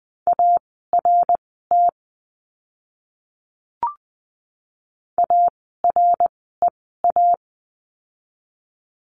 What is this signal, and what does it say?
Morse code, heard as a keyed tone, about 20 wpm, spaced as for 15 wpm, 700 Hz. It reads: ART AREA
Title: 101 Lesson 17f - Letter u - Words 20wpm